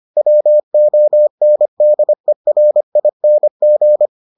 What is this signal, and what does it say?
Morse code, heard as a keyed tone, about 25 wpm, 600 Hz. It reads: WONDERING